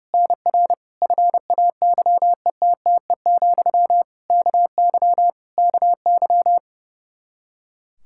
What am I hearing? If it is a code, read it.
Morse code, heard as a keyed tone, about 30 wpm, 700 Hz. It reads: NR FAYETTE, KY KY
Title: Practice QSO - Exchange 1 - Beginning - 30wpm